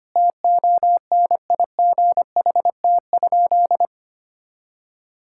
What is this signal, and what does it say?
Morse code, heard as a keyed tone, about 25 wpm, 700 Hz. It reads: TONIGHT?
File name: practice / 25wpm / Sentences with 4-Word Phrases - 25wpm